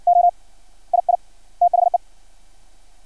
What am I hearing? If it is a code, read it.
Morse code, heard as a keyed tone, about 56 wpm, 700 Hz. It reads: O II THE
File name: archived / CRSnet